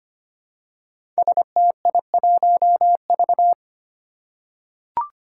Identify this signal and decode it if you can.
Morse code, heard as a keyed tone, about 25 wpm, 700 Hz. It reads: STI1V
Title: Five Challenging Letter-Number Combinations 25wpm